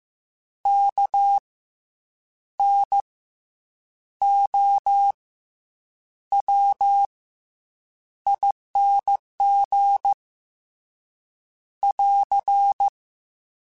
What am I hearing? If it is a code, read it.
Morse code, heard as a keyed tone, about 15 wpm, 800 Hz. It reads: K N O W ING <AR>
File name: Special INGs / Special INGs-mp3s / SlowToFaster ing @13